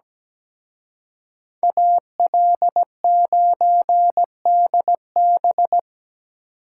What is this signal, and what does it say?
Morse code, heard as a keyed tone, about 20 wpm, 700 Hz. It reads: AL9DB
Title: Speed Racing - USA Call Signs 17wpm